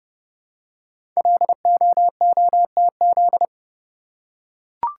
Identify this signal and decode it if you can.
Morse code, heard as a keyed tone, about 28 wpm, 700 Hz. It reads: LOOTZ
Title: Code Groups - 5 Letters - 30wpm